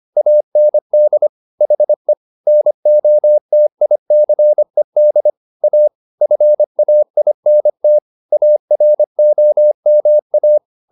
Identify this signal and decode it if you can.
Morse code, heard as a keyed tone, about 25 wpm, 600 Hz. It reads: AND HE NOTICED A FAINT AROMA